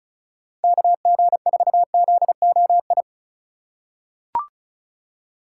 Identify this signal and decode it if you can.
Morse code, heard as a keyed tone, about 36 wpm, 700 Hz. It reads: KG4ZOI
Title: USA Call Signs - 6 Characters - 3x Character Spacing - 35wpm